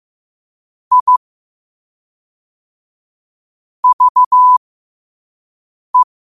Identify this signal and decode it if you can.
Morse code, heard as a keyed tone, about 15 wpm, spaced as for 4 wpm, 1000 Hz. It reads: IVE